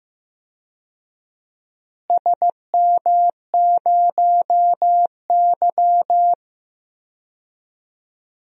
Morse code, heard as a keyed tone, about 15 wpm, 700 Hz. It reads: SM0Y
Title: DXCC-100 Call Signs - 4 Characters - Repeated 2x - 15wpm